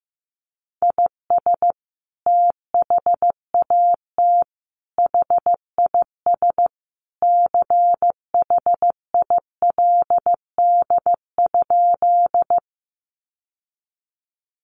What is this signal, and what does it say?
Morse code, heard as a keyed tone, about 15 wpm, 700 Hz. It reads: IS THAT HIS CHILD?
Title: Sentences from Top 800 Words 15wpm